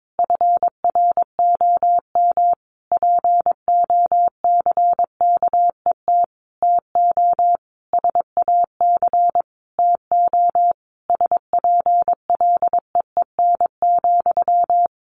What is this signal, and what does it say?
Morse code, heard as a keyed tone, about 22 wpm, 700 Hz. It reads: FROM POCKET TO SAC TO SPLEEN,
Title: A Sound Of Thunder 22wpm